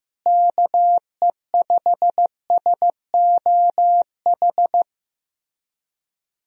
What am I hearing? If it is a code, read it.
Morse code, heard as a keyed tone, about 15 wpm, 700 Hz. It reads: KE5SOH
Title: USA Call Signs - 6 Characters - Repeated 3x - 15wpm